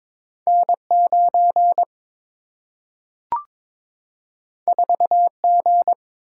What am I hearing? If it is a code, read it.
Morse code, heard as a keyed tone, about 22 wpm, 700 Hz. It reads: N 9 4 G
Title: Single-Letter Single-Number 22wpm